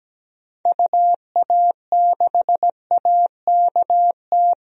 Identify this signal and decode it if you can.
Morse code, heard as a keyed tone, about 17 wpm, 700 Hz. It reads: UA6AKT
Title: DXCC-100 Call Signs - 6 Characters - Repeated 3x - 17wpm